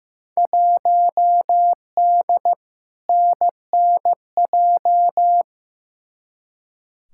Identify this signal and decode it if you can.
Morse code, heard as a keyed tone, about 15 wpm, 700 Hz. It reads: 1D NNJ
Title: Field Day Exchanges - 15wpm